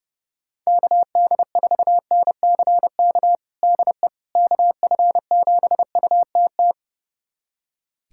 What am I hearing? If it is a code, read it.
Morse code, heard as a keyed tone, about 31 wpm, 700 Hz. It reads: KD4NCK DE KF7UTT
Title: USA Callsign DE Callsign 30wpm